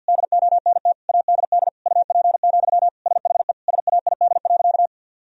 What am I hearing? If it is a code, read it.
Morse code, heard as a keyed tone, about 45 wpm, 700 Hz. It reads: DONT ADD UP, SHE SAID.
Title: Sentences from Top 1000 Words - Review Part 1 - 50wpm